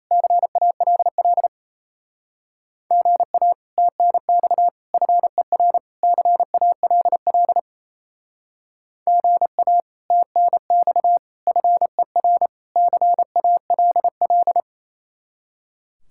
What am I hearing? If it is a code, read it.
Morse code, heard as a keyed tone, about 34 wpm, 700 Hz. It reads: CALL GA TNX FER CALL GA TNX FER CALL
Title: Speed-Racing Common QSO Phrases - 28wpm